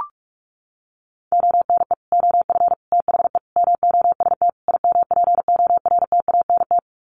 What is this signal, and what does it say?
Morse code, heard as a keyed tone, about 44 wpm, 700 Hz. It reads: ONE OF THE MOST IMPORTANT
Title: Sets of 5 Words 45wpm